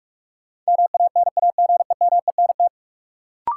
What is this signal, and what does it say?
Morse code, heard as a keyed tone, about 43 wpm, 700 Hz. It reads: MANAGEMENT
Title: Sentences from Top 1200 Words - New Word Drill Alt - 45wpm